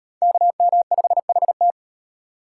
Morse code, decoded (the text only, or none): KM5HT